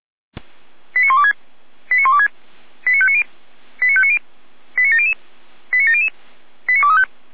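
Five-tone selective call (ZVEI), heard as quick runs of five tones, tones of about 70 ms. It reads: 89126 89126 89590 89590 89700 89700 89235
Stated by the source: ZVEI five-tone selective call received by SDR.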